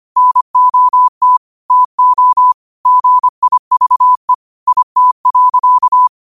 Morse code, heard as a keyed tone, about 25 wpm, 1000 Hz. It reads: NOT TO GIVE IT.
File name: 25wpm